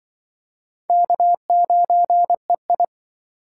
Morse code, heard as a keyed tone, about 24 wpm, 700 Hz. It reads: K9EI
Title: Speed Racing - USA 1x2 Call Signs 20wpm